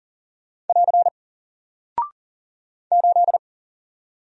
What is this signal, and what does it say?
Morse code, heard as a keyed tone, about 39 wpm, 700 Hz. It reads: <AR> 8
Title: Single Character (Letter:Number:Punctuation:Prosign) Alt 40wpm